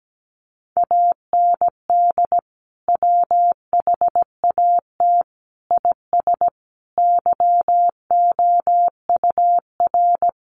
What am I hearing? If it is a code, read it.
Morse code, heard as a keyed tone, about 17 wpm, 700 Hz. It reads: AND WHAT IS YOUR
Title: Sentences from Top 1500 Words 17wpm